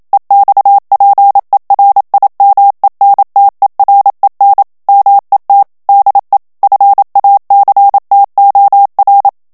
Morse code, heard as a keyed tone, about 27 wpm, 800 Hz. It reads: EXPERIMENTEREN MET DE FACTOR
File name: HOLLANDAIS texte clair 1 vitesse 26 mots minute